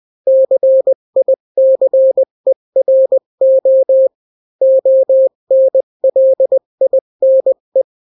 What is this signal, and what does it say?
Morse code, heard as a keyed tone, about 20 wpm, 530 Hz. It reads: CICERO ONLINE
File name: daily / 18wpm